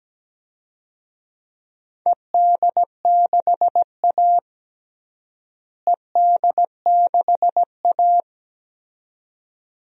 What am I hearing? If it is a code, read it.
Morse code, heard as a keyed tone, about 17 wpm, 700 Hz. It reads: ED6A ED6A
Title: DXCC-100 Call Signs - 4 Characters - Repeated 2x - 17wpm